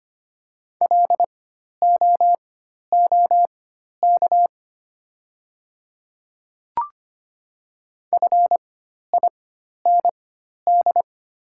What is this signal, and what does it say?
Morse code, heard as a keyed tone, about 25 wpm, spaced as for 12 wpm, 700 Hz. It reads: LOOK FIND
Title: 4x Character Spacing - Top 100 Words - 25wpm